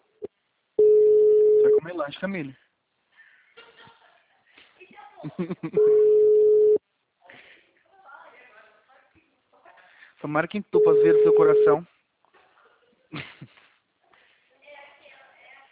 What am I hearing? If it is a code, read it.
Ringback tone (Europe, 425 Hz), with speech.
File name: out- -2014- - - .156